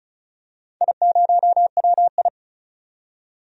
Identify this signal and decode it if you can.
Morse code, heard as a keyed tone, about 34 wpm, 700 Hz. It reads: I0WI
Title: DXCC-100 Call Signs - 4 Characters - 35wpm